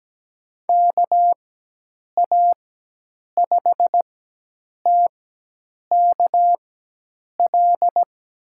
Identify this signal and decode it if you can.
Morse code, heard as a keyed tone, about 17 wpm, spaced as for 8 wpm, 700 Hz. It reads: KA5TKL
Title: USA Call Signs - 6 Characters - 4x Character Spacing - 17wpm